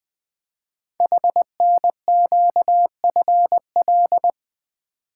Morse code, heard as a keyed tone, about 20 wpm, 700 Hz. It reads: HNQFL
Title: Code Groups - 5 Letters - 20wpm